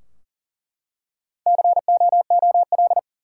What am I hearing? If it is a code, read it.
Morse code, heard as a keyed tone, about 38 wpm, 700 Hz. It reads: COOL
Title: Four Letter Words - 40wpm